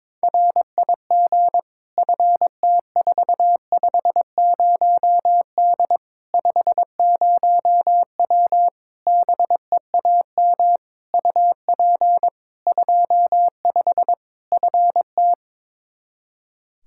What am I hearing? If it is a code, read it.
Morse code, heard as a keyed tone, about 22 wpm, 700 Hz. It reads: RIG FT450D 50W BEAM UP 25 FT
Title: Practice QSO - Exchange 2 - Info - 22wpm